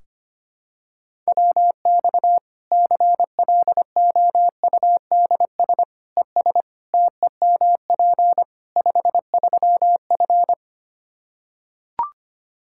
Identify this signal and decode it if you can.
Morse code, heard as a keyed tone, about 25 wpm, 700 Hz. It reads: WX CLOUDS ES TEMP 53F
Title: Practice QSO - Exchange 2 - Info - 25wpm